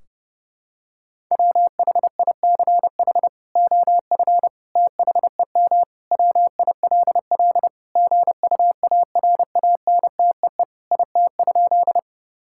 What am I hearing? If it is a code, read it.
Morse code, heard as a keyed tone, about 30 wpm, 700 Hz. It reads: WHICH OF THEM WILL GUARANTEE IT?
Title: Sentences from Top 1600 Words 30wpm